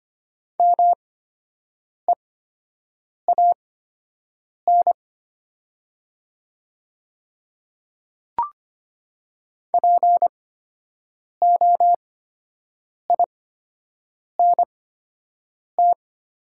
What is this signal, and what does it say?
Morse code, heard as a keyed tone, about 25 wpm, spaced as for 7 wpm, 700 Hz. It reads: MEAN POINT
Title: 8x Character Spacing - Top 200 Words - 25wpm